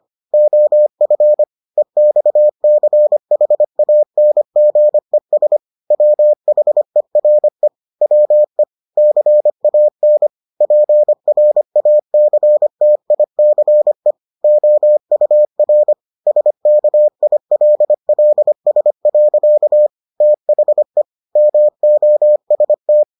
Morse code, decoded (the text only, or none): OF EXCHANGES WHERE WE CAN PRACTICE OUR SKILLS. THE MOST